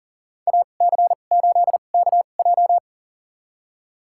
Morse code, heard as a keyed tone, about 40 wpm, spaced as for 29 wpm, 700 Hz. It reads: AC8KJ
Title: USA Call Signs - 2x Character Spacing - 40wpm